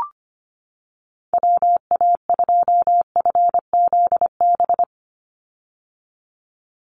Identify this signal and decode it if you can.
Morse code, heard as a keyed tone, about 25 wpm, 700 Hz. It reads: WA2FZB
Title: Call Signs - 25wpm